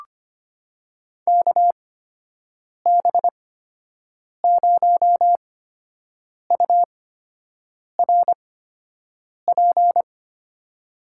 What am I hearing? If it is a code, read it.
Morse code, heard as a keyed tone, about 25 wpm, spaced as for 7 wpm, 700 Hz. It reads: KB0URP
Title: USA Call Signs - 6 Characters - 8x Character Spacing - 25wpm